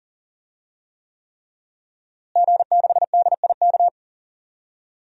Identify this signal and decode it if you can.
Morse code, heard as a keyed tone, about 37 wpm, 700 Hz. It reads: G6DIK
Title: DXCC-100 Call Signs - 5 Characters - Repeated 3x - 40wpm